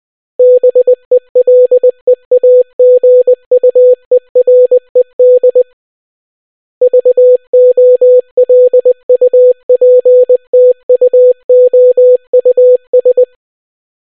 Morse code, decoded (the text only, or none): BELEAGUERED VOLUPTUOUS